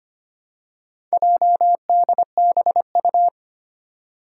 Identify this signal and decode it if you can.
Morse code, heard as a keyed tone, about 25 wpm, 700 Hz. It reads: JDBU
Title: Four Challenging Letter-Number Combinations 25wpm